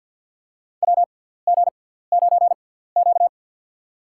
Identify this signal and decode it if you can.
Morse code, heard as a keyed tone, about 53 wpm, spaced as for 17 wpm, 700 Hz. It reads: WG9Q
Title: USA Call Signs - 4 Characters - 6x Character Spacing - 50wpm